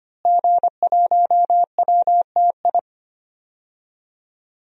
Morse code, heard as a keyed tone, about 25 wpm, 700 Hz. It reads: G1WTI
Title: DXCC-100 Call Signs - 5 Characters - Repeated 2x - 25wpm